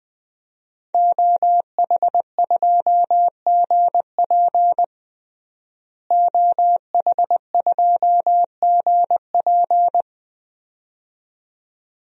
Morse code, heard as a keyed tone, about 21 wpm, 700 Hz. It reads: OH2GP OH2GP
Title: DXCC-100 Call Signs - 5 Characters - Repeated 2x - 20wpm